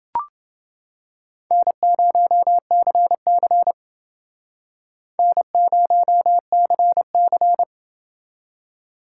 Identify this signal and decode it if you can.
Morse code, heard as a keyed tone, about 30 wpm, 700 Hz. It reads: N0CC N0CC